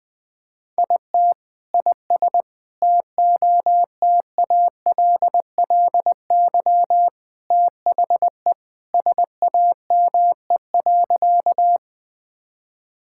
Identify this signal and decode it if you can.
Morse code, heard as a keyed tone, about 20 wpm, 700 Hz. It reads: IT IS TOTALLY THE SAME.
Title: Sentences from Top 1600 Words 20wpm